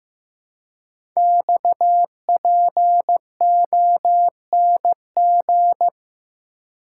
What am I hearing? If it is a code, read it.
Morse code, heard as a keyed tone, about 15 wpm, 700 Hz. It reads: XPONG
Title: Code Groups - 5 Letters - 15wpm